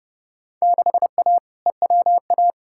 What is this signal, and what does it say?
Morse code, heard as a keyed tone, about 30 wpm, 700 Hz. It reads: 6A EWA
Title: Field Day Exchanges Encore - 30wpm